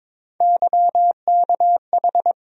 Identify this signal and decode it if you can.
Morse code, heard as a keyed tone, about 22 wpm, 700 Hz. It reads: YKH
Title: USA Call Sign Postfixes - 3 Letters 22wpm